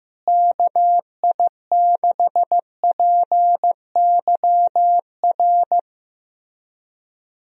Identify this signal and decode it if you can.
Morse code, heard as a keyed tone, about 15 wpm, 700 Hz. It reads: KI6PYR
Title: USA Call Signs - 6 Characters - Repeated 2x - 15wpm